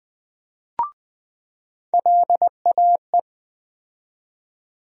Morse code, heard as a keyed tone, about 20 wpm, 700 Hz. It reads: LAE
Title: USA Call Sign Postfixes - 3 Letters 20wpm